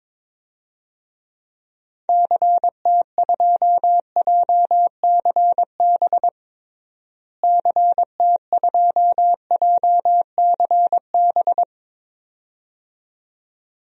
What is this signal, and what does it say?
Morse code, heard as a keyed tone, about 22 wpm, 700 Hz. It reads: CT2JCB CT2JCB
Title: DXCC-100 Call Signs - 6 Characters - Repeated 2x - 22wpm